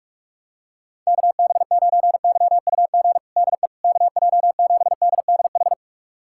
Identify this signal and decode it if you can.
Morse code, heard as a keyed tone, about 46 wpm, 700 Hz. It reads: KB9YUG DE KJ7DDH
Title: USA Callsign DE Callsign - 6 Characters - 45wpm